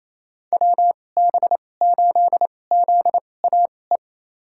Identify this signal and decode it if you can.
Morse code, heard as a keyed tone, about 28 wpm, spaced as for 20 wpm, 700 Hz. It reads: WB8ZAE
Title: USA Call Signs - 6 Characters - 2x Character Spacing - 28wpm